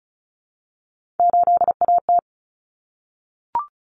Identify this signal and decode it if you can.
Morse code, heard as a keyed tone, about 34 wpm, 700 Hz. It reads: 8AT
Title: Common Cut Numbers - 3-Digit 35wpm